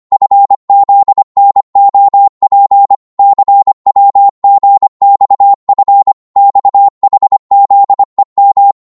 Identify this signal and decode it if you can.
Morse code, heard as a keyed tone, about 25 wpm, 800 Hz. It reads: FZNOP CWGXF XHZEM
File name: Morse / ltr-25wpm